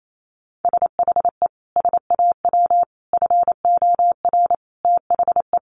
Morse code, heard as a keyed tone, about 28 wpm, 700 Hz. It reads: SHE SAW FOR THE
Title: Sentences from Top 1200 Words 28wpm